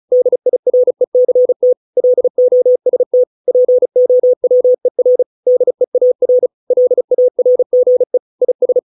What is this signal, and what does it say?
Morse code, heard as a keyed tone, about 34 wpm, 500 Hz. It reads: DIRECT LOST POWER DEAR LARGE IS